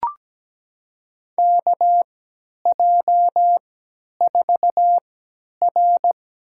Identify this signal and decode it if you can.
Morse code, heard as a keyed tone, about 17 wpm, spaced as for 10 wpm, 700 Hz. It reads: KJ4R
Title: USA Call Signs - 6 Characters - 3x Character Spacing - 17wpm